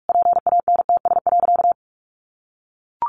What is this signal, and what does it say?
Morse code, heard as a keyed tone, about 44 wpm, 700 Hz. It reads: PANTS.
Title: Additional Easy Phrases 45wpm